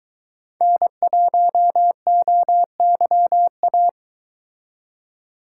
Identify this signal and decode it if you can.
Morse code, heard as a keyed tone, about 22 wpm, 700 Hz. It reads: N1OYA